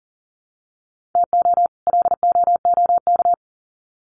Morse code, heard as a keyed tone, about 40 wpm, 700 Hz. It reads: TO LOOK
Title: Sets of 2 Words 40wpm